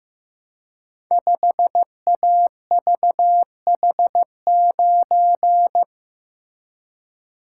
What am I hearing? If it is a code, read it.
Morse code, heard as a keyed tone, about 15 wpm, 700 Hz. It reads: 5AVH9
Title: Code Groups - 5 Letter:Numbers - 15wpm